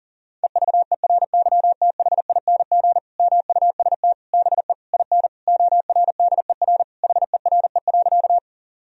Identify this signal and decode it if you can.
Morse code, heard as a keyed tone, about 40 wpm, 700 Hz. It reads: EVERYTHING MUST BE IN ORDER HERE.
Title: Sentences from Top 1000 Words - Review Part 2 - 40wpm